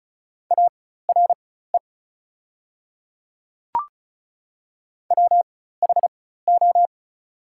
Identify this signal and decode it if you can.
Morse code, heard as a keyed tone, about 36 wpm, spaced as for 16 wpm, 700 Hz. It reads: ARE WHO
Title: 4x Character Spacing - Top 100 Words - 35wpm